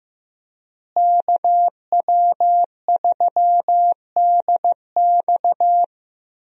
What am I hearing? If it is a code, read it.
Morse code, heard as a keyed tone, about 16 wpm, 700 Hz. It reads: KW3DX
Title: USA Call Signs - 5 Characters - Repeated 3x - 15wpm